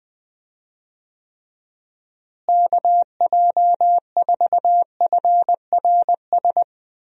Morse code, heard as a keyed tone, about 19 wpm, 700 Hz. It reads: KJ4FRS
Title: USA Call Signs - 6 Characters - Repeated 3x - 20wpm